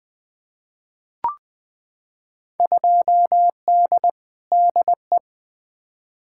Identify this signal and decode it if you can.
Morse code, heard as a keyed tone, about 20 wpm, 700 Hz. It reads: 2D DE
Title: Field Day Exchanges Encore - 20wpm